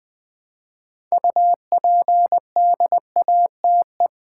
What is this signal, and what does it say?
Morse code, heard as a keyed tone, about 20 wpm, 690 Hz. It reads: UPDATE
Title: Build A Word - 2 Letters At A Time - 6-Letter Words - 20wpm